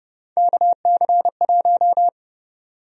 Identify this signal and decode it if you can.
Morse code, heard as a keyed tone, about 32 wpm, 700 Hz. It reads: KC1
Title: USA Call Sign Prefixes - 30wpm